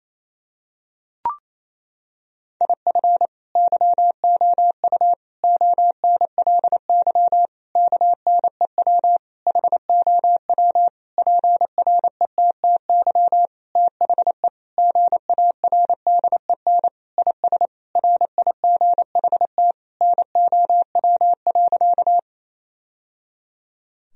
Morse code, heard as a keyed tone, about 28 wpm, 700 Hz. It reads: IF YOU ONLY KNEW HOW PRETTY THE GARDEN IS RIGHT NOW.
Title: Sentences from Top 1000 Words - Review Part 4 - 28wpm